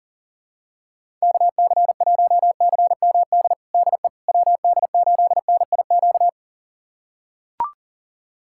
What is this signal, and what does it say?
Morse code, heard as a keyed tone, about 40 wpm, 700 Hz. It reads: KC1CMD DE WD8NIQ
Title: USA Callsign DE Callsign 40wpm